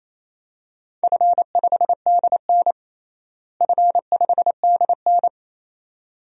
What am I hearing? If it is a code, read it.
Morse code, heard as a keyed tone, about 28 wpm, 700 Hz. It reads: F5DN F5DN
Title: DXCC-100 Call Signs - 4 Characters - Repeated 2x - 28wpm